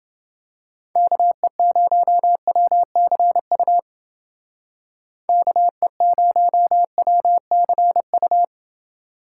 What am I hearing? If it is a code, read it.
Morse code, heard as a keyed tone, about 29 wpm, 700 Hz. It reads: KE0WCU KE0WCU